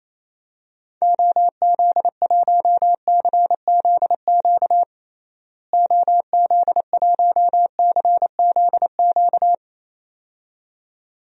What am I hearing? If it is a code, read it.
Morse code, heard as a keyed tone, about 28 wpm, 700 Hz. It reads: OZ1CZQ OZ1CZQ
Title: DXCC-100 Call Signs - 6 Characters - Repeated 2x - 28wpm